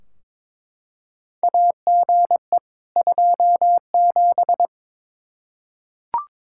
Morse code, heard as a keyed tone, about 21 wpm, 700 Hz. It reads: AGE 27
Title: K1USN SST QSO Practice 22wpm